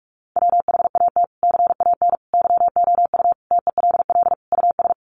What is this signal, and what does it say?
Morse code, heard as a keyed tone, about 44 wpm, 700 Hz. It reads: WHAT CAN YOU TELL US
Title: Sets of 5 Words Encore 45wpm